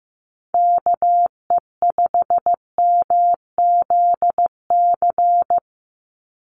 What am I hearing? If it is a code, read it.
Morse code, heard as a keyed tone, about 15 wpm, 700 Hz. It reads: KE5MZC
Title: USA Call Signs 15wpm